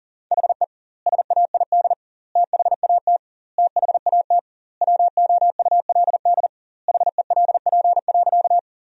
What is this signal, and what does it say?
Morse code, heard as a keyed tone, about 40 wpm, 700 Hz. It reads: HE SAID THAT THAT WOULD HELP.
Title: 2x Word-Spacing - Sentences from Top 200 Words - 40wpm